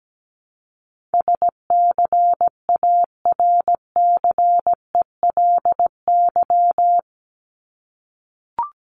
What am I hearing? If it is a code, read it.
Morse code, heard as a keyed tone, about 17 wpm, 700 Hz. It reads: SCARCELY
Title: LY Suffix - Top 100 Words with Base Words 17wpm